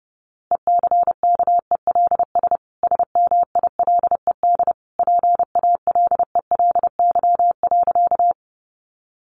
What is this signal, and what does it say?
Morse code, heard as a keyed tone, about 30 wpm, 700 Hz. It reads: ECKELS SMILED PALELY.
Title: A Sound Of Thunder 30wpm